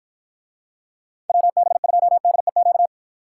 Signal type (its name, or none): Morse code